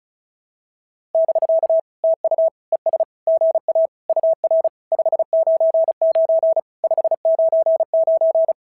Morse code, heard as a keyed tone, about 36 wpm, 690 Hz. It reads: <BK> TU ES GA UR 599 599
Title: SOTA QSO Practice 35wpm